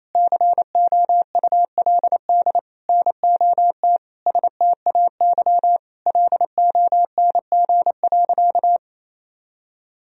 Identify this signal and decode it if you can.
Morse code, heard as a keyed tone, about 27 wpm, 700 Hz. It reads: COULD NOT STAY LONG.
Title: Sentences from Top 1000 Words - Review Part 3 - 28wpm